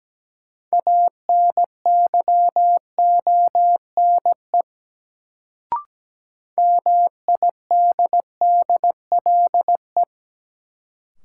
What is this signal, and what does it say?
Morse code, heard as a keyed tone, about 17 wpm, 700 Hz. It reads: ANYONE MIDDLE
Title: Sentences from Top 900 Words - New Word Drill - 17wpm